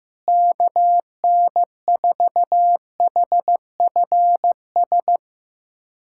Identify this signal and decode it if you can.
Morse code, heard as a keyed tone, about 16 wpm, 700 Hz. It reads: KN4HFS
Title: USA Callsign DE Callsign 15wpm